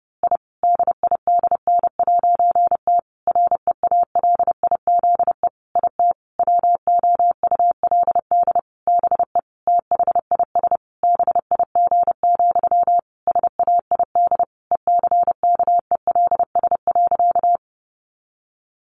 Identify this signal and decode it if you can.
Morse code, heard as a keyed tone, about 30 wpm, 700 Hz. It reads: I DIDN'T REALIZE IT WOULD BE THIS BIG, SAID ECKELS.
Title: A Sound Of Thunder 30wpm